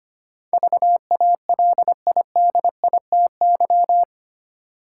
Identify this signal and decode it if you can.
Morse code, heard as a keyed tone, about 25 wpm, 700 Hz. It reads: VALIDITY
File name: Eight Letter Words - 25wpm